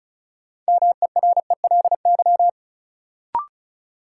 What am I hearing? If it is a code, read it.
Morse code, heard as a keyed tone, about 36 wpm, 700 Hz. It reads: MERELY
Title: Sentences from Top 1400 Words - New Word Drill Alt - 35wpm